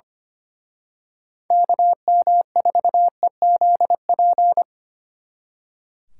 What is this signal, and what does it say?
Morse code, heard as a keyed tone, about 29 wpm, 700 Hz. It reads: KM4EZP
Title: Speed Racing - USA 2x3 Call Signs 25wpm